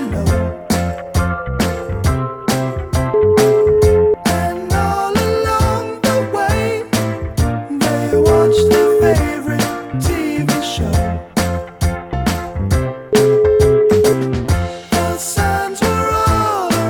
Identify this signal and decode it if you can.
Ringback tone (Europe, 425 Hz), with noise.